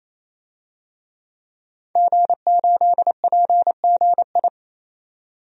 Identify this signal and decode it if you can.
Morse code, heard as a keyed tone, about 28 wpm, 700 Hz. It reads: G8PGI
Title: DXCC-100 Call Signs - 5 Characters - 28wpm